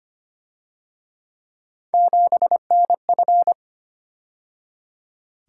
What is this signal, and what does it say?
Morse code, heard as a keyed tone, about 29 wpm, 700 Hz. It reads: 7NF
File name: Speed-Racing - Three Letter-Number Combinations - 25wpm